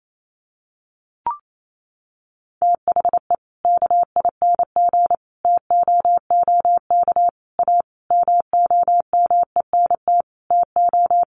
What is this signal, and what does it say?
Morse code, heard as a keyed tone, about 28 wpm, 700 Hz. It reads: THE KING TOOK A MOMENT TO
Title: Sentences from Top 1500 Words 28wpm